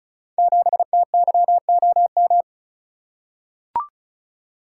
Morse code, heard as a keyed tone, about 36 wpm, 700 Hz. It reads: 7TYOM
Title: Code Groups - 5 Letter:Numbers - 35wpm